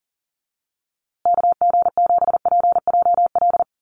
Morse code, heard as a keyed tone, about 40 wpm, 700 Hz. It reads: KG7PJL
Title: USA Call Signs Repeated 3x 40wpm